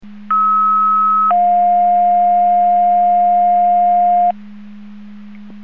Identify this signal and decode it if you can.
Two-tone page, heard as a one-second tone, then a three-second tone, about 1290 then 710 Hz. It is heard from 0.3 to 4.3 s.